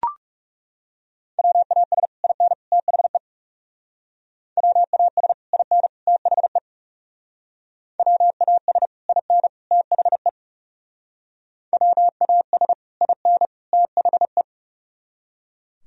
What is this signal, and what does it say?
Morse code, heard as a keyed tone, about 37 wpm, 700 Hz. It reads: WAS IN THE WAS IN THE WAS IN THE WAS IN THE
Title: Speed-Racing - Sets of 3 Words 30wpm